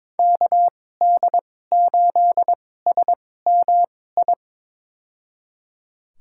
Morse code, heard as a keyed tone, about 22 wpm, spaced as for 16 wpm, 700 Hz. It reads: KD8SMI
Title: USA Call Signs - 6 Characters - 2x Character Spacing - 22wpm